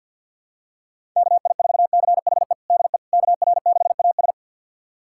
Morse code, heard as a keyed tone, about 48 wpm, 700 Hz. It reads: KI4CHE DE KR6AS
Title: USA Callsign DE Callsign 50wpm